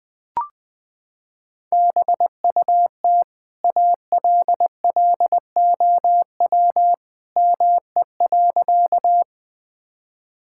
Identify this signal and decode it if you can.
Morse code, heard as a keyed tone, about 20 wpm, 700 Hz. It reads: BUT ALLOW ME.
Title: Sentences from Top 1000 Words - Review Part 5 - 20wpm